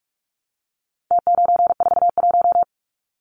Morse code, heard as a keyed tone, about 44 wpm, 700 Hz. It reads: T941
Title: Numbers - Contest Style - 4-Digit 45wpm